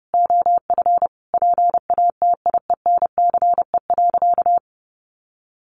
Morse code, heard as a keyed tone, about 30 wpm, 700 Hz. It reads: OF PATIENCE.
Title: Sentences from Top 1300 Words 30wpm